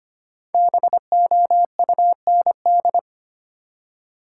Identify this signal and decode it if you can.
Morse code, heard as a keyed tone, about 25 wpm, 700 Hz. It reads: BOUND
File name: 5 Letter Words - First 3 Letters Given - 25wpm